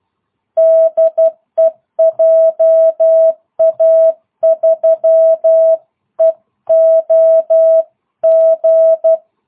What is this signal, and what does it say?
Morse code, heard as a keyed tone, about 12 wpm, 660 Hz. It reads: DE JA3EOG